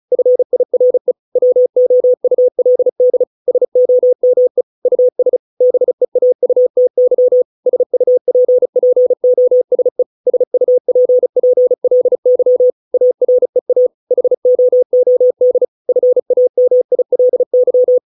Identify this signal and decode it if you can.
Morse code, heard as a keyed tone, about 34 wpm, 500 Hz. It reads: FIRE WOULD SOME US BEAUTY SUPPOSE SUPPLY AREA HOOD FAMILY